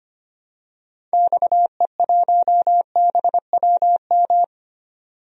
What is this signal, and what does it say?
Morse code, heard as a keyed tone, about 25 wpm, 700 Hz. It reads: XE1BWM
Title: DXCC-100 Call Signs - 6 Characters - 25wpm